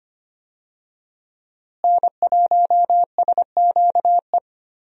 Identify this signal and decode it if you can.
Morse code, heard as a keyed tone, about 25 wpm, 700 Hz. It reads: N1SQE
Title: USA Call Signs - 5 Characters - Repeated 2x - 25wpm